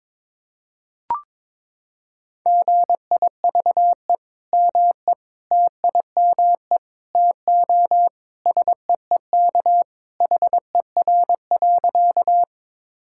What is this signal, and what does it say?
Morse code, heard as a keyed tone, about 22 wpm, 700 Hz. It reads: GIVE ME TIME TO SEEK HER.
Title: Sentences from Top 1600 Words 22wpm